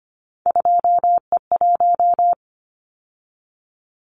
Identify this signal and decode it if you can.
Morse code, heard as a keyed tone, about 25 wpm, 700 Hz. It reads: 2E1
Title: Callsign Prefixes for Top 25 DXCC - 25wpm